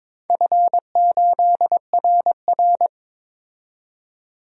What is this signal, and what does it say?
Morse code, heard as a keyed tone, about 23 wpm, 700 Hz. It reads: F8RR
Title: DXCC-100 Call Signs - 4 Characters - 22wpm